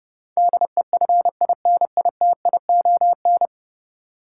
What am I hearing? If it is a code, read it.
Morse code, heard as a keyed tone, about 30 wpm, 700 Hz. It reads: DEFINITION
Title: Sentences from Top 1000 Words - Review Part 4 - New Word Drill - 30wpm